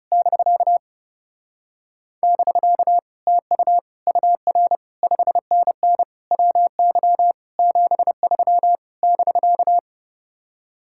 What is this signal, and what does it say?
Morse code, heard as a keyed tone, about 30 wpm, 700 Hz. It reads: <BK> <BK> TU UR 5NN WY 73 <BK>
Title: Speed-Racing Common QSO Phrases - 30wpm